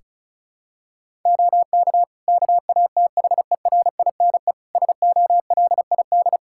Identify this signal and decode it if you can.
Morse code, heard as a keyed tone, about 36 wpm, 700 Hz. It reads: OK KATHERINE SOLID
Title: K1USN SST QSO Practice 35wpm